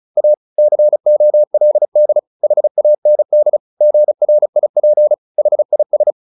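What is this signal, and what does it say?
Morse code, heard as a keyed tone, about 35 wpm, 600 Hz. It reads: A COLD HAND GRIP HIS
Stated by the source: Morse practice at 35 wpm from CW med Gnister, sending Ken Follett's novel Never in English.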